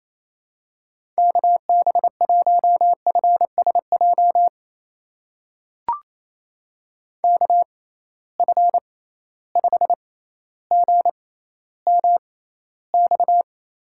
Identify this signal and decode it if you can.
Morse code, heard as a keyed tone, about 28 wpm, spaced as for 14 wpm, 700 Hz. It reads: KB1FSJ KF5GMX
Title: USA Call Signs - 6 Characters - 6x Character Spacing - 28wpm